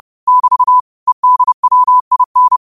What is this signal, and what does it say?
Morse code, heard as a keyed tone, about 30 wpm, 1000 Hz. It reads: <BT> EDWIN
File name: qotdfiles / 30wpm